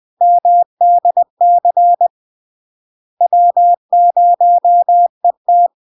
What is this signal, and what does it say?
Morse code, heard as a keyed tone, about 20 wpm, 700 Hz. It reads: MDC W0ET
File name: LARK-FD-2022-Log-in-CW-20wpm-700-Hz